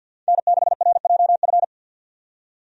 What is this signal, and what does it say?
Morse code, heard as a keyed tone, about 53 wpm, 700 Hz. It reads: N6RJF
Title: USA Call Signs - 5 Characters - 50wpm